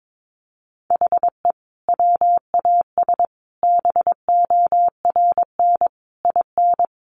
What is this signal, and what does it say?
Morse code, heard as a keyed tone, about 22 wpm, 700 Hz. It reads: HE WAS BORN IN
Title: Sentences from Top 1000 Words 22wpm